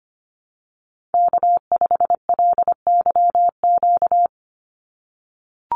Morse code, heard as a keyed tone, about 25 wpm, 690 Hz. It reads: K5LYQ
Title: USA Call Signs 25wpm